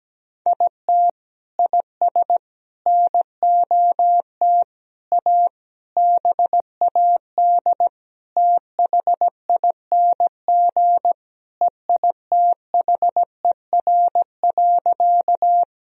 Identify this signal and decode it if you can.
Morse code, heard as a keyed tone, about 17 wpm, 700 Hz. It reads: IT IS NOT A BAD THING EITHER.
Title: Sentences from Top 1000 Words - Review Part 4 - 17wpm